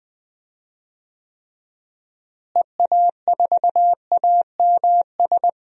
Morse code, heard as a keyed tone, about 20 wpm, 700 Hz. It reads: EA4AMS
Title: DXCC-100 Call Signs - 6 Characters - Repeated 3x - 20wpm